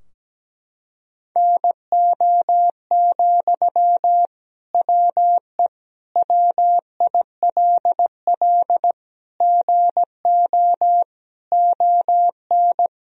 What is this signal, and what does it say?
Morse code, heard as a keyed tone, about 17 wpm, 700 Hz. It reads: NO, WE WILL GO ON
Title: Sentences from Top 1000 Words - Review Part 3 - 17wpm